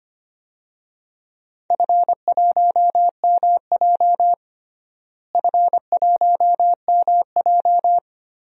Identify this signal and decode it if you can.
Morse code, heard as a keyed tone, about 25 wpm, 700 Hz. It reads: F1MJ F1MJ
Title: DXCC-100 Call Signs - 4 Characters - Repeated 2x - 25wpm